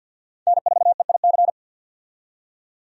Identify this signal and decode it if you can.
Morse code, heard as a keyed tone, about 52 wpm, 700 Hz. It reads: N4EIC
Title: USA Call Signs - 5 Characters - 50wpm